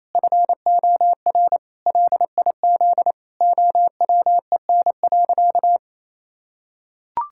Morse code, heard as a keyed tone, about 27 wpm, 700 Hz. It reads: FOR LIZ OWEN.
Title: Pangram Challenge v2 - 28wpm